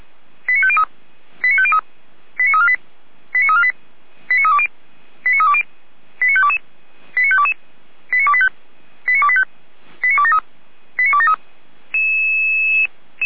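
Five-tone selective call (ZVEI), heard as quick runs of five tones, tones of about 70 ms. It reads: 89503 89503 89348 89348 89320 89320 89522 89522 89285 89285 89273 89273
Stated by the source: ZVEI five-tone call 89010 received on an SDR, logged by feuerwehr-orlamuende.org fire brigade.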